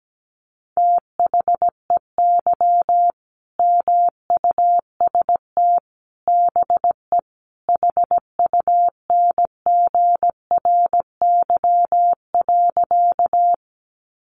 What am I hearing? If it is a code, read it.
Morse code, heard as a keyed tone, about 17 wpm, 700 Hz. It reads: THEY MUST BE HUNGRY.
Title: Sentences from Top 1400 Words 17wpm